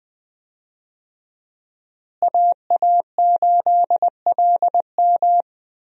Morse code, heard as a keyed tone, about 20 wpm, 700 Hz. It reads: AA8LM
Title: USA Call Signs - 5 Characters - Repeated 2x - 20wpm